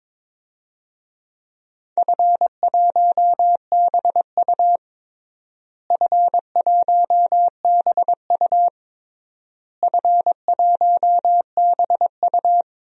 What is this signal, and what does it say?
Morse code, heard as a keyed tone, about 22 wpm, 700 Hz. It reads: F1BU F1BU F1BU
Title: DXCC-100 Call Signs - 4 Characters - Repeated 3x - 22wpm